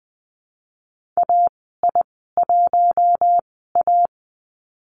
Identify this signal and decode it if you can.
Morse code, heard as a keyed tone, about 20 wpm, spaced as for 14 wpm, 700 Hz. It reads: AI1A
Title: 062 Lesson 11g - Letter h - Call Signs 20wpm